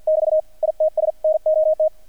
Morse code, heard as a keyed tone, about 48 wpm, 620 Hz. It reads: <BT> ITS NOT
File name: archived / CRSnet